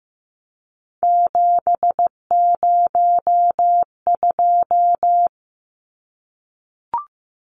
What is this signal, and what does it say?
Morse code, heard as a keyed tone, about 15 wpm, 700 Hz. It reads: 702
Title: Numbers - 3-Digit 15wpm